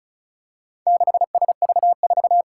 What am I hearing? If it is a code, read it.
Morse code, heard as a keyed tone, about 36 wpm, 700 Hz. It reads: 6SV4
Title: Four Challenging Letter-Number Combinations 35wpm